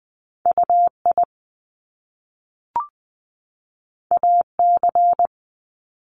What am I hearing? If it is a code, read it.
Morse code, heard as a keyed tone, about 20 wpm, 700 Hz. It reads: UI AC
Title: Top Two Letter Combinations in Words 20wpm